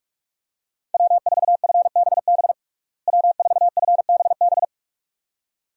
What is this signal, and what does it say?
Morse code, heard as a keyed tone, about 41 wpm, 700 Hz. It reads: W4FBB W4FBB
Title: USA Call Signs - 5 Characters - Repeated 2x - 45wpm